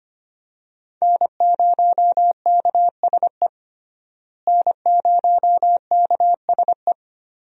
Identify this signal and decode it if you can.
Morse code, heard as a keyed tone, about 25 wpm, 700 Hz. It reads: N0KSE N0KSE
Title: USA Call Signs - 5 Characters - Repeated 2x - 25wpm